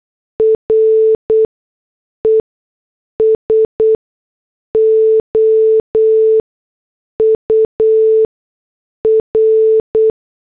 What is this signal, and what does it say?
Morse code, heard as a keyed tone, about 8 wpm, spaced as for 6 wpm, 440 Hz. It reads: RESOUR
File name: morse / headlines